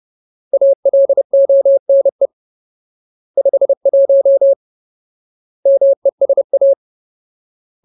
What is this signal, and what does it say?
Morse code, heard as a keyed tone, about 30 wpm, 550 Hz. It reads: ALONE 51 MESA
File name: Ultimate Word List / 30WPM Ultimate